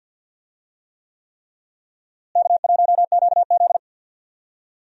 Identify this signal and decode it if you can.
Morse code, heard as a keyed tone, about 50 wpm, 700 Hz. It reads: K1QZ
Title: USA Call Signs - 4 Characters - Repeated 3x - 50wpm